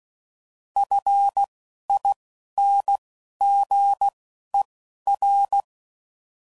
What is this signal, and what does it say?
Morse code, heard as a keyed tone, about 16 wpm, spaced as for 12 wpm, 800 Hz. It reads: FINGER